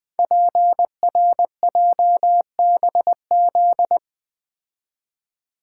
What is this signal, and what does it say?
Morse code, heard as a keyed tone, about 20 wpm, 700 Hz. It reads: PRJBZ
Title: Code Groups - 5 Letters - 20wpm